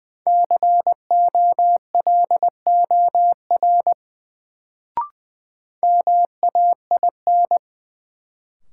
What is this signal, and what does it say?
Morse code, heard as a keyed tone, about 20 wpm, 700 Hz. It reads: COLOR MAIN
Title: Sentences from Top 1000 Words - Review Part 2 - New Word Drill - 20wpm